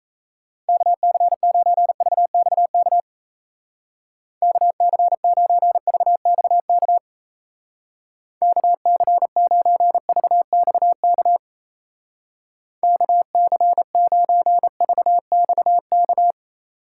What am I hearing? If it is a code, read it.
Morse code, heard as a keyed tone, about 36 wpm, 700 Hz. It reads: KC9VXK KC9VXK KC9VXK KC9VXK